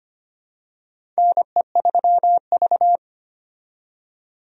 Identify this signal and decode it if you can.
Morse code, heard as a keyed tone, about 25 wpm, 700 Hz. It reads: NE3V
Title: USA Call Signs - 4 Characters - 25wpm